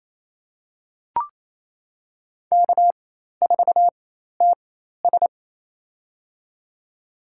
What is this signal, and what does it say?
Morse code, heard as a keyed tone, about 28 wpm, spaced as for 13 wpm, 700 Hz. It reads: K4TS
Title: USA Call Signs - 4 Characters - 4x Character Spacing - 28wpm